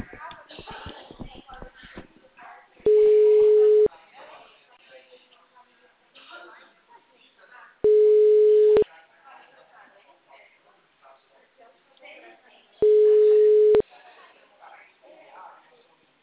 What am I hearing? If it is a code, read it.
Ringback tone (Europe, 425 Hz), with speech.